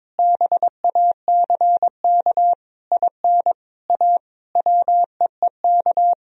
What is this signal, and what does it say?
Morse code, heard as a keyed tone, about 22 wpm, 700 Hz. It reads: BACK IN A WEEK
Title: Sentences from Top 1000 Words - Review Part 2 - 22wpm